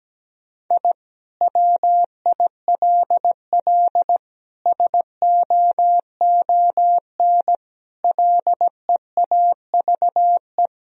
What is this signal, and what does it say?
Morse code, heard as a keyed tone, about 17 wpm, 700 Hz. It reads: I WILL SOON LEAVE
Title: Sentences from Top 1000 Words - Review Part 2 - 17wpm